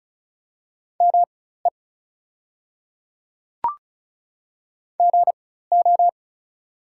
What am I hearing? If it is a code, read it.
Morse code, heard as a keyed tone, about 36 wpm, spaced as for 16 wpm, 700 Hz. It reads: ME GO
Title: 4x Character Spacing - Top 300 Words - 35wpm